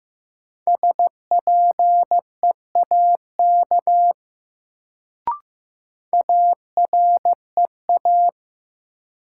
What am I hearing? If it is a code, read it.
Morse code, heard as a keyed tone, about 15 wpm, 700 Hz. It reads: SPEAK AREA
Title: Sentences from Top 1000 Words - Review Part 2 - New Word Drill - 15wpm